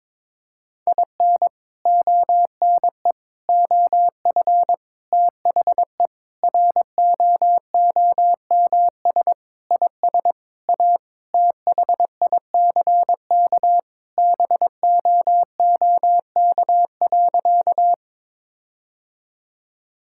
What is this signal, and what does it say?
Morse code, heard as a keyed tone, about 22 wpm, 700 Hz. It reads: IN ONE OF THE ROOMS IS A THICK BOOK.
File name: practice / 22wpm / Sentences with 4-Word Phrases - 22wpm